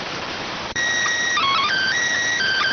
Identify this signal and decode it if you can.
SSTV (slow-scan television), PD 290.